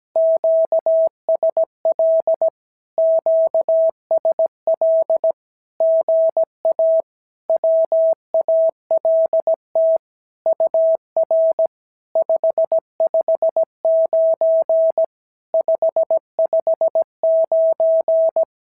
Morse code, heard as a keyed tone, about 17 wpm, 650 Hz. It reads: QSL QSL GA WALT UR 559 559
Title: SOTA QSO Practice 17wpm